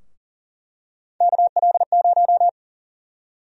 Morse code, heard as a keyed tone, about 40 wpm, 700 Hz. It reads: KL0
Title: USA Call Sign Prefixes - Repeated 2x - 40wpm